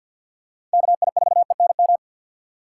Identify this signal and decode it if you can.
Morse code, heard as a keyed tone, about 52 wpm, 700 Hz. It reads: KI4ENM
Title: USA Call Signs - 6 Characters - 6x Character Spacing - 50wpm